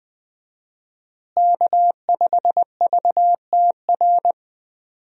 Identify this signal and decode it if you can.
Morse code, heard as a keyed tone, about 21 wpm, 700 Hz. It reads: K5VTR
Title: USA Call Signs - 5 Characters - Repeated 2x - 20wpm